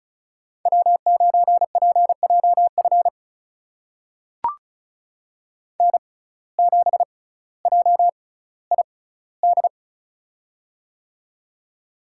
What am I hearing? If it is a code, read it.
Morse code, heard as a keyed tone, about 36 wpm, spaced as for 18 wpm, 700 Hz. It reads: W9PJF N7JID
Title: USA Call Signs - 5 Characters - 6x Character Spacing - 35wpm